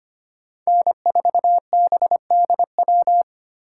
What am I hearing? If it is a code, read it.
Morse code, heard as a keyed tone, about 31 wpm, 700 Hz. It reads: N4BDW
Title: Speed Racing - USA Call Signs 25wpm